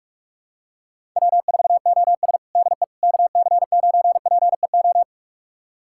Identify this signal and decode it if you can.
Morse code, heard as a keyed tone, about 46 wpm, 700 Hz. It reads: W4OS DE KC9PEO
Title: USA Callsign DE Callsign 45wpm